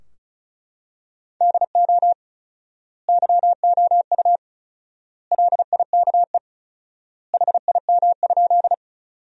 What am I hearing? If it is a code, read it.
Morse code, heard as a keyed tone, about 34 wpm, 700 Hz. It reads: DO YOU LIKE HIM?
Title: 4x Word-Spacing - Sentences from Top 100 Words - 35wpm